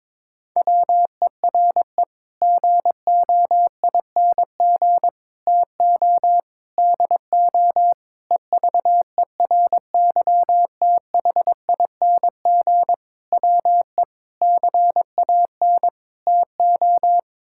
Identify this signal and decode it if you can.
Morse code, heard as a keyed tone, about 22 wpm, 700 Hz. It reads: WERE GOING TO DO EVERYTHING WE CAN TO
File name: practice / 22wpm / Sets of 8 Words - 22wpm